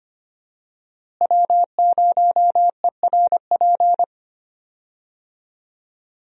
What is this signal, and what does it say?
Morse code, heard as a keyed tone, about 25 wpm, 700 Hz. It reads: W0ERP
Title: USA Call Signs - 5 Characters - Repeated 2x - 25wpm